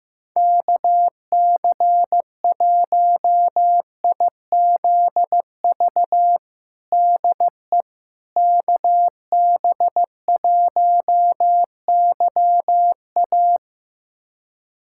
Morse code, heard as a keyed tone, about 15 wpm, 700 Hz. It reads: KC1IZV DE KB1YA
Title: USA Callsign DE Callsign 15wpm